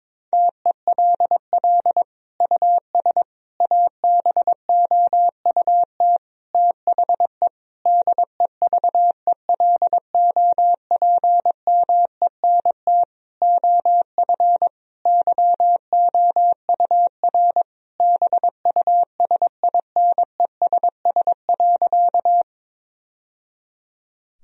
Morse code, heard as a keyed tone, about 21 wpm, 700 Hz. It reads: TELL US ABOUT THE DEVELOPMENT OF YOUR BUSINESS.
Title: Sentences from Top 1000 Words - Review Part 4 - 22wpm